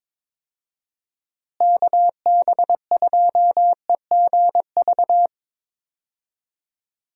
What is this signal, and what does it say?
Morse code, heard as a keyed tone, about 23 wpm, 700 Hz. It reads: KB2EGV
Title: USA Call Signs - 6 Characters - Repeated 3x - 22wpm